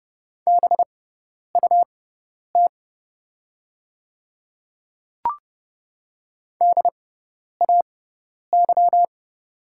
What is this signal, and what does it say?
Morse code, heard as a keyed tone, about 30 wpm, spaced as for 10 wpm, 700 Hz. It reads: BUT DAY
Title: 6x Character Spacing - Top 100 Words - 30wpm